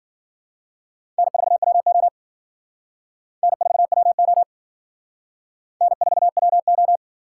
Morse code, heard as a keyed tone, about 51 wpm, 700 Hz. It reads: N4WO N4WO N4WO